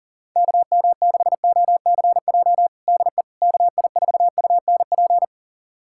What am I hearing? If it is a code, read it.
Morse code, heard as a keyed tone, about 40 wpm, 700 Hz. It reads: KM6OCJ DE KI4UNP
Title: USA Callsign DE Callsign - 6 Characters - 40wpm